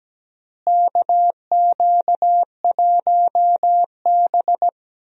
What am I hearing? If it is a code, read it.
Morse code, heard as a keyed tone, about 17 wpm, 700 Hz. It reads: KQ1B
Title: USA Call Signs - 4 Characters - Repeated 2x - 17wpm